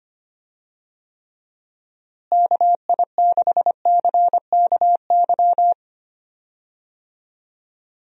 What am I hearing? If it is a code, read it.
Morse code, heard as a keyed tone, about 25 wpm, 700 Hz. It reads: KI6CKY
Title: USA Call Signs - 6 Characters - Repeated 3x - 25wpm